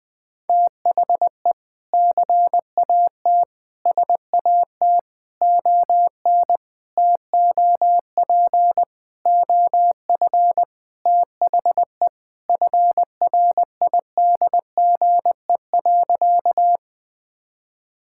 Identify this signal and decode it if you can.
Morse code, heard as a keyed tone, about 20 wpm, 700 Hz. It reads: THE CAT SAT ON TOP OF THE FRIDGE.
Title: Sentences with 4-Word Phrases - 20wpm